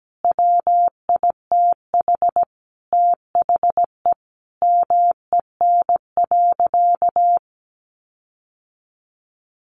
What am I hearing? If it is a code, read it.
Morse code, heard as a keyed tone, about 17 wpm, 700 Hz. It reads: WITH THE MEN.
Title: Sentences from Top 1400 Words 17wpm